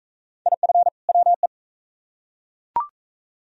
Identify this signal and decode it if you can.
Morse code, heard as a keyed tone, about 40 wpm, 700 Hz. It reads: IF WE